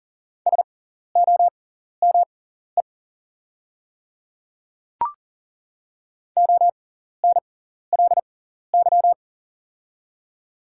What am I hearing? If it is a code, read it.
Morse code, heard as a keyed tone, about 42 wpm, spaced as for 14 wpm, 700 Hz. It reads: SOME ONLY
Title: 6x Character Spacing - Top 200 Words - 40wpm